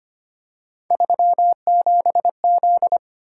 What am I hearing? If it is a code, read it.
Morse code, heard as a keyed tone, about 25 wpm, 700 Hz. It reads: 37Z
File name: Three Letter-Number Combinations - 25wpm